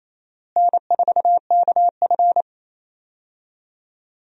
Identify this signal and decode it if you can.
Morse code, heard as a keyed tone, about 28 wpm, 700 Hz. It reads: N4KF
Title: USA Call Signs - 4 Characters - Repeated 2x - 28wpm